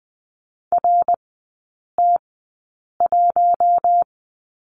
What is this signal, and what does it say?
Morse code, heard as a keyed tone, about 20 wpm, spaced as for 8 wpm, 700 Hz. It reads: RT1
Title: 074 Lesson 13e - Letter L - Three Characters 20wpm